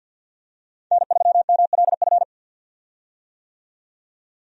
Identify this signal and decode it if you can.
Morse code, heard as a keyed tone, about 50 wpm, 700 Hz. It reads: N3MLF
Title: USA Call Signs - 5 Characters - 50wpm